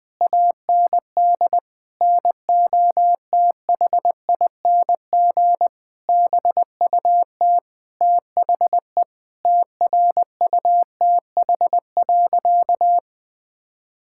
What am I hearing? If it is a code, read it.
Morse code, heard as a keyed tone, about 20 wpm, 700 Hz. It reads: AND NOTHING BUT THE TRUTH.
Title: Sentences with 8-Word Phrases - 20wpm